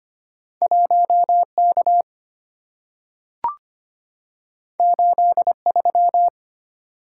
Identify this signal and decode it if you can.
Morse code, heard as a keyed tone, about 25 wpm, 700 Hz. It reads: 1K 83
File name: Two Letter-Number Combinations - 25wpm